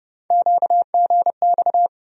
Morse code, heard as a keyed tone, about 30 wpm, 700 Hz. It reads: QGX
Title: USA Call Sign Postfixes - 3 Letters 30wpm